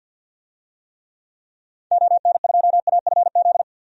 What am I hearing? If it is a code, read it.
Morse code, heard as a keyed tone, about 51 wpm, 700 Hz. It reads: ON2AFZ
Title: DXCC-100 Call Signs - 6 Characters - 50wpm